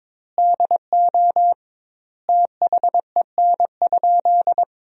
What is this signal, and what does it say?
Morse code, heard as a keyed tone, about 22 wpm, 700 Hz. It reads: DO THEN?
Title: 2x Word-Spacing - Sentences from Top 100 Words - 22wpm